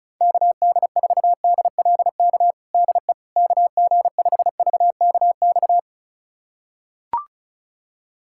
Morse code, heard as a keyed tone, about 36 wpm, 700 Hz. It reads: KD4DLK DE KG5VKX
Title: USA Callsign DE Callsign - 6 Characters - 35wpm